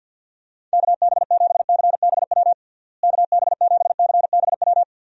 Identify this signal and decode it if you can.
Morse code, heard as a keyed tone, about 51 wpm, 700 Hz. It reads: KB7CBW KB7CBW
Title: USA Call Signs - 6 Characters - Repeated 3x - 50wpm